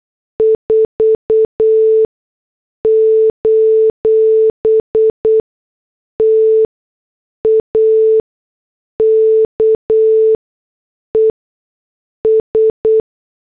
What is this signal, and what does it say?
Morse code, heard as a keyed tone, about 8 wpm, spaced as for 6 wpm, 440 Hz. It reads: 4: TAKES
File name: morse / headlines